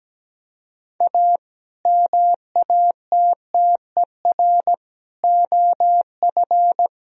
Morse code, heard as a keyed tone, about 17 wpm, 700 Hz. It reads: A MATTER OF
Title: Speed-Racing - Sets of 4 Words 17wpm